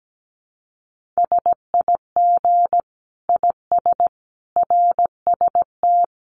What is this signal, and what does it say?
Morse code, heard as a keyed tone, about 17 wpm, 700 Hz. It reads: SIG IS RST
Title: Common QSO Phrases 17wpm